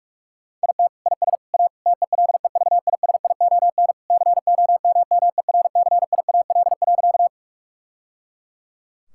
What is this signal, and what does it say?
Morse code, heard as a keyed tone, about 46 wpm, 700 Hz. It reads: IT IS A TELEVISION COMMERCIAL.
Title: Sentences from Top 1000 Words - Review Part 5 - 45wpm